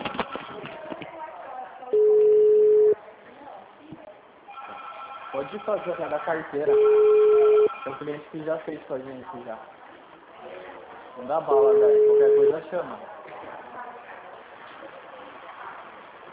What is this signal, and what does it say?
Ringback tone (Europe, 425 Hz), with speech.
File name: out- -2013- - - .75551